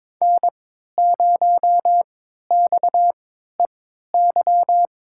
Morse code, heard as a keyed tone, about 21 wpm, spaced as for 12 wpm, 700 Hz. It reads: N0XEY
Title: USA Call Signs - 5 Characters - 3x Character Spacing - 22wpm